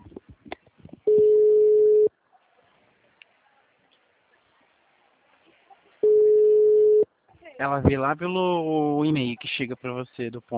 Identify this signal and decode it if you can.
Ringback tone (Europe, 425 Hz), with speech.